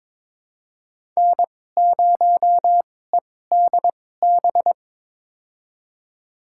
Morse code, heard as a keyed tone, about 22 wpm, spaced as for 16 wpm, 700 Hz. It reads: N0EDB
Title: USA Call Signs - 5 Characters - 2x Character Spacing - 22wpm